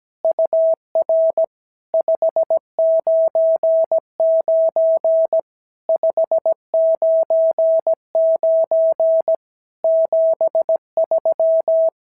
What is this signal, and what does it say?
Morse code, heard as a keyed tone, about 17 wpm, 650 Hz. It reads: UR 599 599 73
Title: SOTA QSO Practice 17wpm